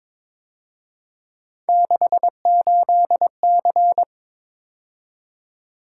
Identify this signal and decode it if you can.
Morse code, heard as a keyed tone, about 26 wpm, 700 Hz. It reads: 68C
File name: Speed-Racing - Three Letter-Number Combinations - 22wpm